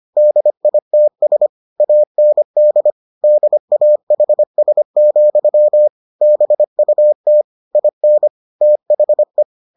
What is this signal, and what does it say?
Morse code, heard as a keyed tone, about 25 wpm, 600 Hz. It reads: DITS AND DAHS, BUT IN THE